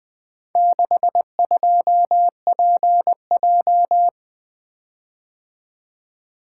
Morse code, heard as a keyed tone, about 20 wpm, 700 Hz. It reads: 62PJ
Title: Four Letter-Number Combinations - 20wpm